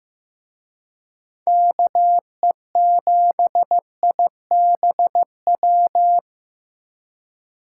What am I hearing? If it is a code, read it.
Morse code, heard as a keyed tone, about 18 wpm, 700 Hz. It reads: KE7IBW
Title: Speed Racing - USA 2x3 Call Signs 15wpm